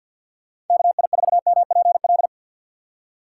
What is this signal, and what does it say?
Morse code, heard as a keyed tone, about 50 wpm, 700 Hz. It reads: KI4MPL
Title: USA Call Signs - 6x Character Spacing - 50wpm